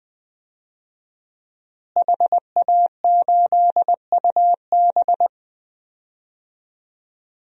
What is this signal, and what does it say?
Morse code, heard as a keyed tone, about 20 wpm, 700 Hz. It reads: HA8UB
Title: DXCC-100 Call Signs - 5 Characters - Repeated 3x - 20wpm